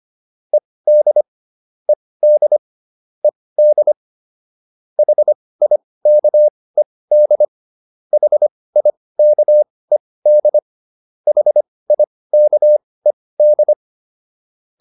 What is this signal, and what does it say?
Morse code, heard as a keyed tone, about 25 wpm, spaced as for 18 wpm, 600 Hz. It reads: ED ED ED HIKED HIKED HIKED